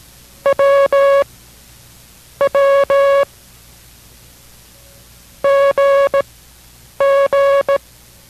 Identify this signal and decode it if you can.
Morse code, heard as a keyed tone, about 17 wpm, spaced as for 7 wpm, 560 Hz. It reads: WW GG